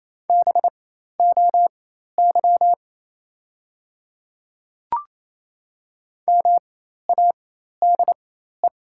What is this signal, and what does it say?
Morse code, heard as a keyed tone, about 28 wpm, spaced as for 13 wpm, 700 Hz. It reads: BOY MADE
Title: 4x Character Spacing - Top 200 Words - 28wpm